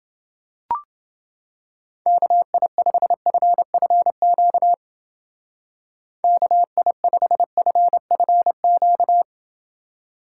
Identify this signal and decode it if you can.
Morse code, heard as a keyed tone, about 29 wpm, 700 Hz. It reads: KI5FFQ KI5FFQ